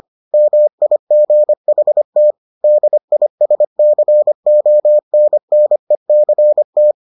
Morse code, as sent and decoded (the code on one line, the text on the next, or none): -- .. --. .... - / -.. .. ... -.-. --- -. -. -.-. -
MIGHT DISCONNECT